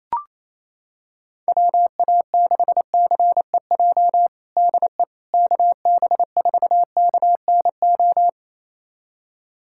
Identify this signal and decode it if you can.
Morse code, heard as a keyed tone, about 28 wpm, 700 Hz. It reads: WA6CEJ DE KB4KNO
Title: USA Callsign DE Callsign - 6 Characters - 28wpm